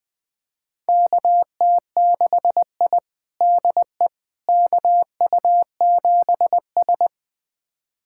Morse code, heard as a keyed tone, about 20 wpm, 700 Hz. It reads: KT6I DE KU7S
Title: USA Callsign DE Callsign - 4 Characters - 20wpm